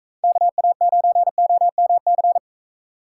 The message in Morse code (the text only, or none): KA9OMC